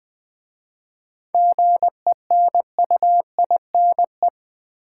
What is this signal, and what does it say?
Morse code, heard as a keyed tone, about 20 wpm, 700 Hz. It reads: GENUINE
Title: 7 Letter Words - First 3 Letters Given - 20wpm